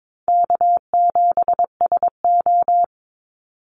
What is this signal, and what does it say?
Morse code, heard as a keyed tone, about 22 wpm, 700 Hz. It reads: K7SO
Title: NA SOTA Activator Callsigns 22wpm